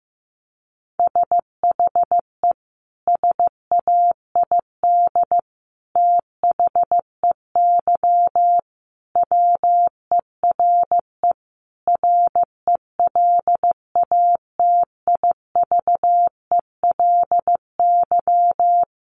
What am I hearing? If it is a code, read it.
Morse code, heard as a keyed tone, about 15 wpm, 700 Hz. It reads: SHE SAID THEY WERE RELATIVELY
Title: Sentences from Top 1100 Words 15wpm